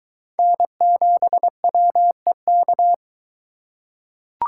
Morse code, heard as a keyed tone, about 23 wpm, 700 Hz. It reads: N7WEK